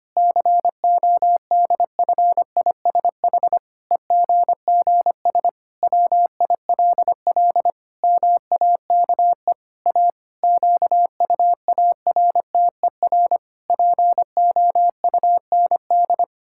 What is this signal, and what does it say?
Morse code, heard as a keyed tone, about 25 wpm, 700 Hz. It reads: CODFISH EGGS WILL MAKE A QUARTER POUND
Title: Pangram Challenge v2 - 25wpm